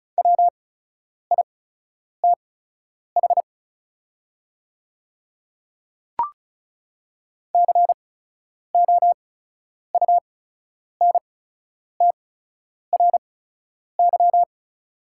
Morse code, heard as a keyed tone, about 34 wpm, spaced as for 10 wpm, 690 Hz. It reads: WITH COUNTRY
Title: 8x Character Spacing - Top 500 Words - 35wpm